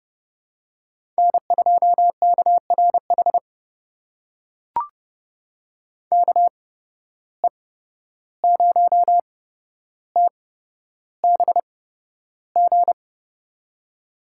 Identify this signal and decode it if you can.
Morse code, heard as a keyed tone, about 29 wpm, spaced as for 12 wpm, 700 Hz. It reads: N2KRH KE0TBG
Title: USA Call Signs - 8x Character Spacing - 30wpm